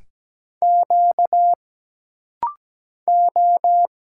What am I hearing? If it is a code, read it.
Morse code, heard as a keyed tone, about 17 wpm, 700 Hz. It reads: Q O
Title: Single Letter-Number Alt - Rapid-Fire 17wpm